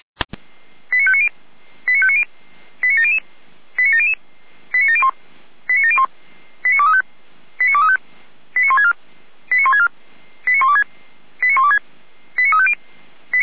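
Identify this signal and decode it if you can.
Five-tone selective call (ZVEI), heard as quick runs of five tones, tones of about 70 ms. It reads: 89590 89590 89700 89700 89701 89701 89235 89235 89164 89164 89126 89126 89350
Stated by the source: ZVEI five-tone selective call code 89115, received by SDR.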